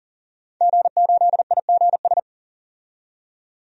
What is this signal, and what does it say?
Morse code, heard as a keyed tone, about 37 wpm, 700 Hz. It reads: G8IGS
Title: DXCC-100 Call Signs - 5 Characters - 40wpm